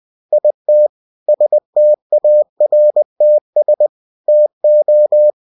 Morse code, heard as a keyed tone, about 20 wpm, 600 Hz. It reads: IT STARTS TO
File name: cw-books / winnie-the-pooh / winnie-20wpm-page-0068